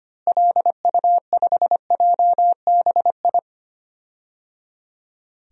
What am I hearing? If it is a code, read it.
Morse code, heard as a keyed tone, about 25 wpm, 700 Hz. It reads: LU5JBI
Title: DXCC-100 Call Signs - 6 Characters - Repeated 3x - 25wpm